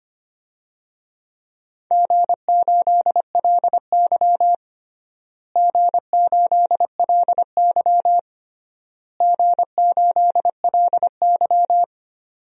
Morse code, heard as a keyed tone, about 25 wpm, 700 Hz. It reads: G8LY G8LY G8LY
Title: DXCC-100 Call Signs - 4 Characters - Repeated 3x - 25wpm